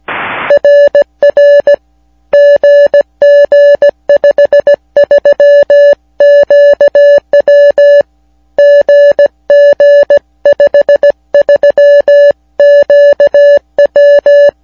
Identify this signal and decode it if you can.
Morse code, heard as a keyed tone, about 17 wpm, 580 Hz. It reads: RR GG53QW GG53QW